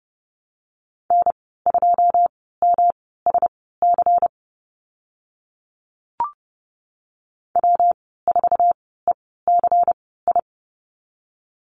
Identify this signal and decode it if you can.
Morse code, heard as a keyed tone, about 30 wpm, spaced as for 17 wpm, 700 Hz. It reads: N2MSC W4ECI
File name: Lesson 19g - Letter w - Call Signs 30wpm